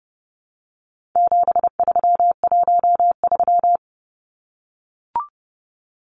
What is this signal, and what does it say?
Morse code, heard as a keyed tone, about 30 wpm, 700 Hz. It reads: 7313
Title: Numbers - 4-Digit 30wpm